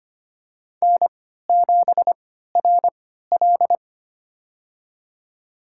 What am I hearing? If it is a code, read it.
Morse code, heard as a keyed tone, about 25 wpm, spaced as for 14 wpm, 700 Hz. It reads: N7RL
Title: USA Call Signs - 4 Characters - 3x Character Spacing - 25wpm